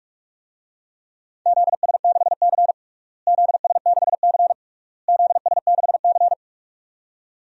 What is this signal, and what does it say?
Morse code, heard as a keyed tone, about 46 wpm, 700 Hz. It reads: ZS6C ZS6C ZS6C
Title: DXCC-100 Call Signs - 4 Characters - Repeated 3x - 45wpm